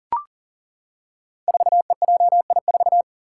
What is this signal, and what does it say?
Morse code, heard as a keyed tone, about 37 wpm, 700 Hz. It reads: 4EJI4
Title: Five Challenging Letter-Number Combinations 40wpm